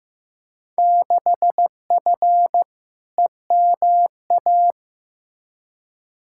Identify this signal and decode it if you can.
Morse code, heard as a keyed tone, about 15 wpm, 700 Hz. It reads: 6F EMA
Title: Field Day Exchanges Encore - 15wpm